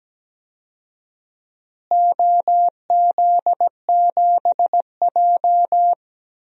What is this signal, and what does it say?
Morse code, heard as a keyed tone, about 17 wpm, 700 Hz. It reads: OZ7J
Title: DXCC-100 Call Signs - 4 Characters - Repeated 2x - 17wpm